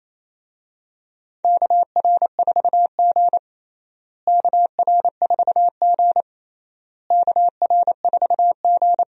Morse code, heard as a keyed tone, about 28 wpm, 700 Hz. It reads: KR4G KR4G KR4G
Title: USA Call Signs - 4 Characters - Repeated 3x - 28wpm